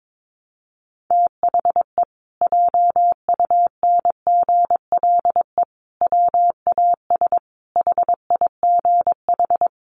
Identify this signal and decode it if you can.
Morse code, heard as a keyed tone, about 22 wpm, 700 Hz. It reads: THE JUNGLE WAS HIGH
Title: A Sound Of Thunder 22wpm